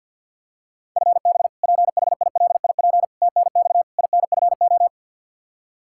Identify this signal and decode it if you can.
Morse code, heard as a keyed tone, about 47 wpm, 700 Hz. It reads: FB PHILIP TNX INFO
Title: Practice QSO - Exchange 3 - More Info - 50wpm